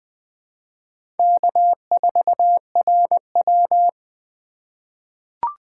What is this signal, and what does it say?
Morse code, heard as a keyed tone, about 20 wpm, 700 Hz. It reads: K4RW
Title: USA Call Signs - 4x Character Spacing - 20wpm